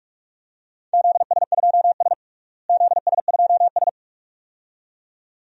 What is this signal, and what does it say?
Morse code, heard as a keyed tone, about 43 wpm, 700 Hz. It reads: ZS2S ZS2S
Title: DXCC-100 Call Signs - 4 Characters - Repeated 2x - 45wpm